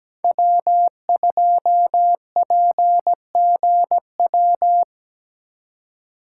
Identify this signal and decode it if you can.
Morse code, heard as a keyed tone, about 20 wpm, 700 Hz. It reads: W2PGW
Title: Speed Racing - USA 1x3 Call Signs 17wpm